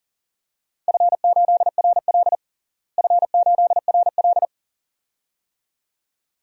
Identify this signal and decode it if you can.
Morse code, heard as a keyed tone, about 40 wpm, 700 Hz. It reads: F8RL F8RL
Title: DXCC-100 Call Signs - 4 Characters - Repeated 2x - 40wpm